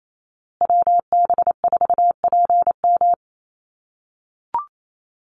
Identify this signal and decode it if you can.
Morse code, heard as a keyed tone, about 28 wpm, 700 Hz. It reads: WB4PM
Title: USA Call Signs Repeated 2x 28wpm